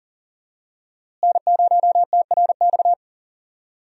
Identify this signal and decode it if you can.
Morse code, heard as a keyed tone, about 40 wpm, 700 Hz. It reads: N0TRX
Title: USA Call Signs - 5 Characters - Repeated 2x - 40wpm